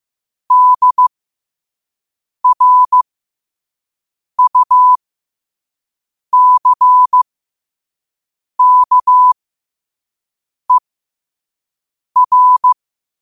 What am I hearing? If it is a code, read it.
Morse code, heard as a keyed tone, about 15 wpm, spaced as for 5 wpm, 1000 Hz. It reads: DRUCKER
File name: qotdfiles / 5wpm